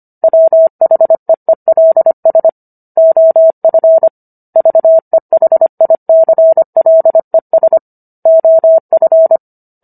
Morse code, heard as a keyed tone, about 25 wpm, 650 Hz. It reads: WHEELS OF VEHICLES OF